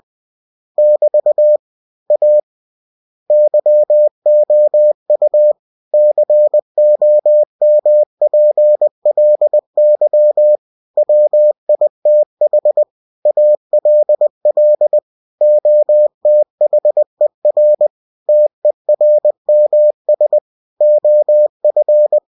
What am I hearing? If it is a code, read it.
Morse code, heard as a keyed tone, about 20 wpm, 600 Hz. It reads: <BT> A YOU COMPLY WITH ALL OTHER TERMS OF
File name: cw-books / winnie-the-pooh / winnie-20wpm-page-0117